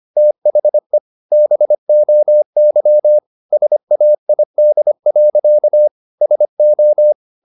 Morse code, heard as a keyed tone, about 25 wpm, 600 Hz. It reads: THE BOY SAID. SO
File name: the old man and the sea part 1 25wpm